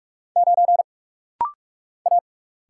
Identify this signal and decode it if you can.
Morse code, heard as a keyed tone, about 46 wpm, 700 Hz. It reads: 9 A
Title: Single Letter-Number - Rapid-Fire 45wpm